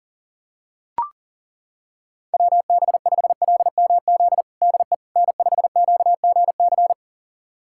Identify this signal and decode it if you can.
Morse code, heard as a keyed tone, about 40 wpm, 700 Hz. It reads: WB5LMZ DE N5QGC
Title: USA Callsign DE Callsign 40wpm